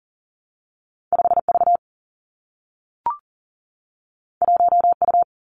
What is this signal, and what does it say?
Morse code, heard as a keyed tone, about 38 wpm, 700 Hz. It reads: 5 V 1 U
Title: Two Challenging Letter-Number Combinations 40wpm